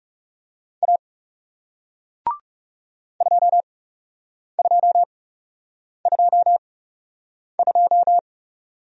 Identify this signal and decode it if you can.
Morse code, heard as a keyed tone, about 41 wpm, spaced as for 8 wpm, 700 Hz. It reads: A 2222